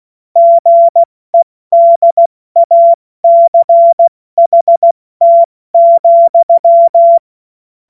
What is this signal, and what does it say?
Morse code, heard as a keyed tone, about 16 wpm, 680 Hz. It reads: GEDACHT,